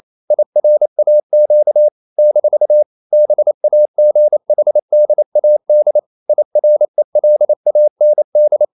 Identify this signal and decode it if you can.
Morse code, heard as a keyed tone, about 28 wpm, 600 Hz. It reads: IRAQ - BAGHDAD IRELAND
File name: CW / Practice / NationalCapitals28